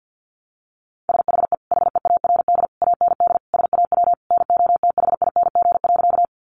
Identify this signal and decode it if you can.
Morse code, heard as a keyed tone, about 50 wpm, 700 Hz. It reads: SHE HEARD AND SAW NOTHING.
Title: Sentences from Top 500 Words 50wpm